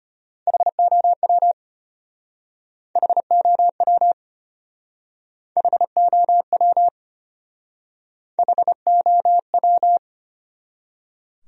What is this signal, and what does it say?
Morse code, heard as a keyed tone, about 32 wpm, 700 Hz. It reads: HOW HOW HOW HOW